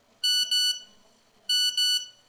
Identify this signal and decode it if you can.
There is no code.